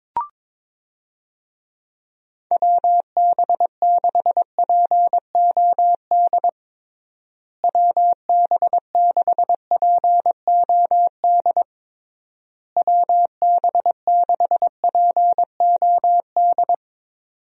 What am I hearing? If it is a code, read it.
Morse code, heard as a keyed tone, about 22 wpm, 700 Hz. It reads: WB6POD WB6POD WB6POD
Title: USA Call Signs - 6 Characters - Repeated 3x - 22wpm